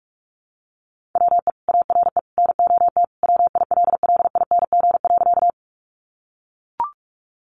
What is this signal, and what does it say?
Morse code, heard as a keyed tone, about 44 wpm, 700 Hz. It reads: WE ARE NOT WILLING.
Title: Sentences from Top 1000 Words 45wpm